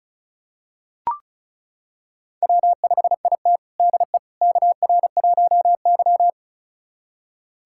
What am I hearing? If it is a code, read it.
Morse code, heard as a keyed tone, about 36 wpm, 700 Hz. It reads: W5IT DE KR1Y
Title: USA Callsign DE Callsign - 4 Characters - 35wpm